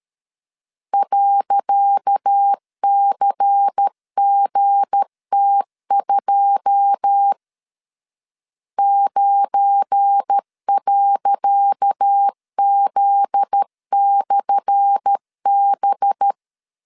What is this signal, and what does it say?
Morse code, heard as a keyed tone, about 13 wpm, 800 Hz. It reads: .CGT2 9.Z/B
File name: LEZIONI / 10 SIGNS / 10C DOT 12wpm 2451